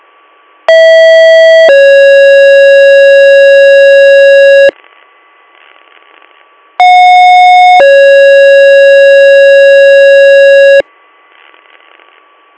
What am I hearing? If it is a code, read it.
Two-tone page, heard as a one-second tone, then a three-second tone, about 650 then 550 Hz, about 730 then 550 Hz, with noise. It is heard from 0.7 to 10.8 s.